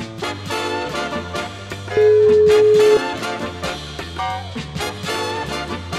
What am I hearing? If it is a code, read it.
Ringback tone (Europe, 425 Hz).